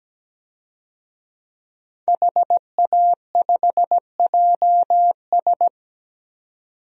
Morse code, heard as a keyed tone, about 17 wpm, 700 Hz. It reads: HA5JS
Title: DXCC-100 Call Signs - 5 Characters - 17wpm